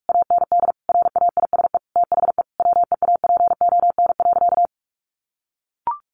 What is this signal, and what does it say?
Morse code, heard as a keyed tone, about 46 wpm, 700 Hz. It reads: AND RAISE THE WEAPON.
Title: A Sound Of Thunder 45wpm